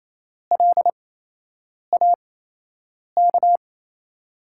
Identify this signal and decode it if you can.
Morse code, heard as a keyed tone, about 28 wpm, spaced as for 8 wpm, 700 Hz. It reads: LAK
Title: USA Call Signs - 5 Characters - 8x Character Spacing - 28wpm